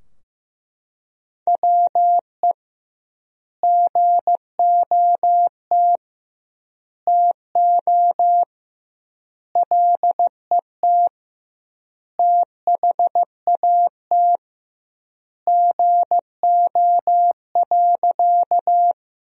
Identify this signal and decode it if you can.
Morse code, heard as a keyed tone, about 15 wpm, 700 Hz. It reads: WE GOT TO LET THAT GO.
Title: 2x Word-Spacing - Sentences from Top 300 Words - 15wpm